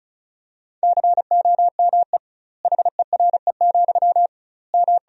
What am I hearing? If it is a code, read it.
Morse code, heard as a keyed tone, about 36 wpm, 700 Hz. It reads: COME HERE, M
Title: 2x Word-Spacing - Sentences from Top 300 Words - 35wpm